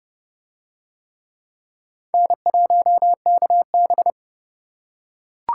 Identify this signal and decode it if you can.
Morse code, heard as a keyed tone, about 29 wpm, 700 Hz. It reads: N1KB